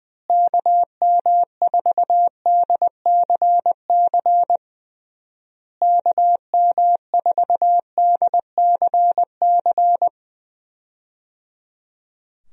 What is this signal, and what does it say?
Morse code, heard as a keyed tone, about 20 wpm, 700 Hz. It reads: KM4DCC KM4DCC
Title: USA Call Signs - 6 Characters - Repeated 3x - 20wpm